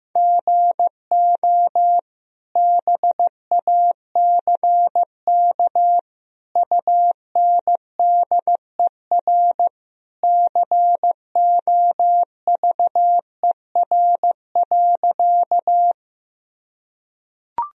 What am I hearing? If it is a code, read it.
Morse code, heard as a keyed tone, about 15 wpm, 700 Hz. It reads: GO BACK UNDER COVER.
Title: Sentences from Top 1000 Words - Review Part 1 - 15wpm